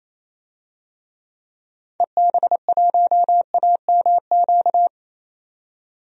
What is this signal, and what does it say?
Morse code, heard as a keyed tone, about 27 wpm, 700 Hz. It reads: EB1AMQ
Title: DXCC-100 Call Signs - 6 Characters - 28wpm